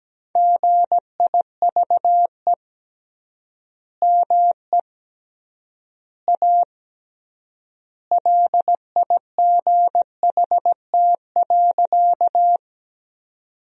Morse code, heard as a keyed tone, about 17 wpm, 700 Hz. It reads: GIVE ME A LIGHT.
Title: 3x Word-Spacing - Sentences from Top 200 Words - 17wpm